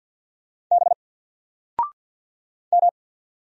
Morse code, heard as a keyed tone, about 50 wpm, 700 Hz. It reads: B M
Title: Single Letter-Number 50wpm